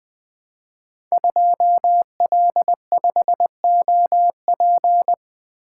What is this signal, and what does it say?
Morse code, heard as a keyed tone, about 20 wpm, 700 Hz. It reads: 2L5OP
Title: Code Groups - 5 Letter:Numbers - 20wpm